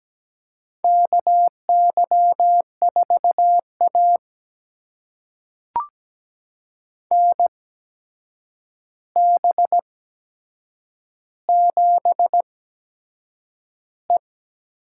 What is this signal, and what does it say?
Morse code, heard as a keyed tone, about 17 wpm, spaced as for 7 wpm, 700 Hz. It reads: KY4A NB7E
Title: USA Call Signs - 4 Characters - 8x Character Spacing - 17wpm